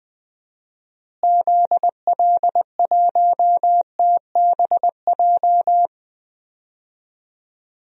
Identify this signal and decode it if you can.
Morse code, heard as a keyed tone, about 21 wpm, 700 Hz. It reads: ZL1TBJ
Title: DXCC-100 Call Signs - 6 Characters - Repeated 2x - 20wpm